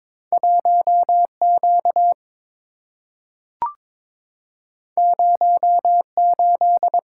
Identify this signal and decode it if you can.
Morse code, heard as a keyed tone, about 22 wpm, 690 Hz. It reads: 1Q 08
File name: practice / 22wpm / Two Letter-Number Combinations - 22wpm